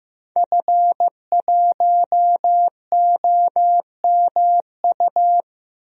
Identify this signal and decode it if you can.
Morse code, heard as a keyed tone, about 15 wpm, 700 Hz. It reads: F1OMU
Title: DXCC-100 Call Signs - 5 Characters - 15wpm